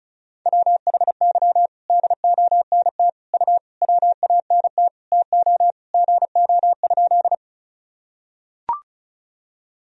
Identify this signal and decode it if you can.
Morse code, heard as a keyed tone, about 36 wpm, 700 Hz. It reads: WHY DONT U WANT TO GO?
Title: Sentences from Top 300 Words and Abbreviations 35wpm